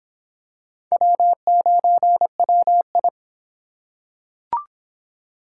Morse code, heard as a keyed tone, about 26 wpm, 700 Hz. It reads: W9WI